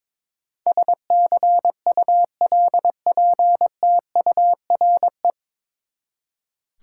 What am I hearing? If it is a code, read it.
Morse code, heard as a keyed tone, about 22 wpm, 700 Hz. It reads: SCULPTURE
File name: practice / 22wpm / Nine Letter Words - 22wpm